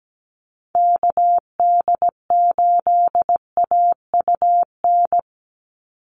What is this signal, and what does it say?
Morse code, heard as a keyed tone, about 17 wpm, 700 Hz. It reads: KD8AUN
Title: USA Call Signs 17wpm